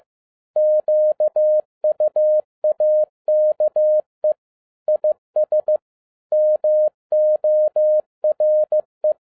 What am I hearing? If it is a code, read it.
Morse code, heard as a keyed tone, about 15 wpm, 610 Hz. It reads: QUAKE IS MORE
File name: 2011-10-mp3 / n02-FIELD DAY CONTEST QRV FROM SAKAE VILLAGE 15wpm